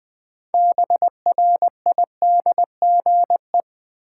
Morse code, heard as a keyed tone, about 20 wpm, 700 Hz. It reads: BRIDGE
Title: Sentences from Top 1200 Words - New Word Drill Alt - 20wpm